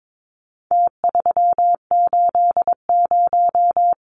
Morse code, heard as a keyed tone, about 22 wpm, 700 Hz. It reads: T380
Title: Numbers - Contest Style - 4-Digit 22wpm